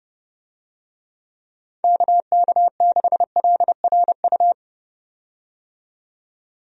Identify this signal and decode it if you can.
Morse code, heard as a keyed tone, about 30 wpm, 700 Hz. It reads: KK6LRU
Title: USA Call Signs - 6 Characters - 30wpm